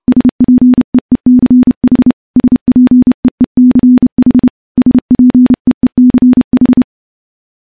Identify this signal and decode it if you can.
Morse code, heard as a keyed tone, about 30 wpm, 250 Hz. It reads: SPEECH SPEECH SPEECH
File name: morse / morse-training-30WPM